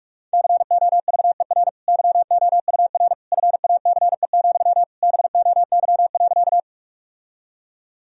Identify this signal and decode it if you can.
Morse code, heard as a keyed tone, about 46 wpm, 700 Hz. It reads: COVER YOUR FACE, BOY.
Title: Sentences from Top 1000 Words - Review Part 1 - 45wpm